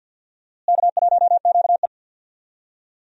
Morse code, heard as a keyed tone, about 46 wpm, 700 Hz. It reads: K1QE
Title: USA Call Signs - 4 Characters - Repeated 2x - 50wpm